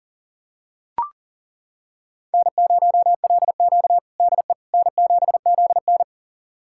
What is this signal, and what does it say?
Morse code, heard as a keyed tone, about 40 wpm, 700 Hz. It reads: N0LQ DE N7ZN
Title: USA Callsign DE Callsign - 4 Characters - 40wpm